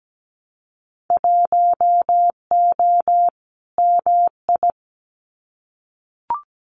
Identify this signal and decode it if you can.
Morse code, heard as a keyed tone, about 17 wpm, 700 Hz. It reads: 1O MI
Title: Winter Field Day Exchanges Encore 17wpm